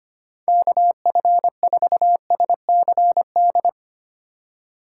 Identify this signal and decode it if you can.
Morse code, heard as a keyed tone, about 25 wpm, 700 Hz. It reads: KF4SCD
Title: USA Call Signs - 6 Characters - 2x Character Spacing - 25wpm